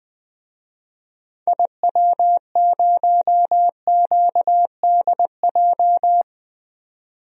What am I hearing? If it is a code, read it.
Morse code, heard as a keyed tone, about 21 wpm, 700 Hz. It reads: IW0QDJ
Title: DXCC-100 Call Signs - 6 Characters - 20wpm